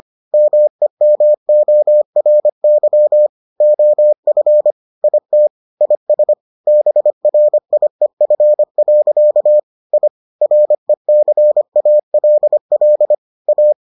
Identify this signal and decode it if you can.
Morse code, heard as a keyed tone, about 25 wpm, 600 Hz. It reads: MEMORY OF IT IS BRIEF. I RECALL A